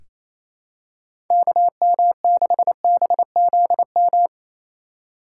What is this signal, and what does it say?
Morse code, heard as a keyed tone, about 28 wpm, 700 Hz. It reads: KM6BZM
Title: USA Call Signs - 4x Character Spacing - 28wpm